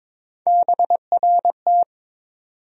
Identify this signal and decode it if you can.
Morse code, heard as a keyed tone, about 22 wpm, 700 Hz. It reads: BRT
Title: USA Call Sign Postfixes - 3 Letters 22wpm